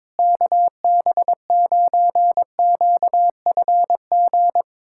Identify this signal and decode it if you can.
Morse code, heard as a keyed tone, about 21 wpm, 700 Hz. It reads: KB9QFG
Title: USA Callsign DE Callsign - 6 Characters - 22wpm